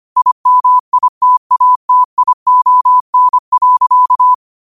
Morse code, heard as a keyed tone, about 25 wpm, 1000 Hz. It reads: IMITATION.